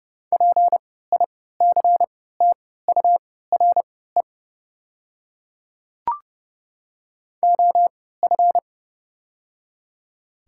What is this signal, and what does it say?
Morse code, heard as a keyed tone, about 30 wpm, spaced as for 17 wpm, 700 Hz. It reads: PICTURE OF
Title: 3x Character Spacing - Top 200 Words - 30wpm